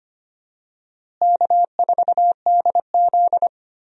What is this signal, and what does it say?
Morse code, heard as a keyed tone, about 25 wpm, 700 Hz. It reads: K4DZ
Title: USA Call Signs - 4 Characters - Repeated 2x - 25wpm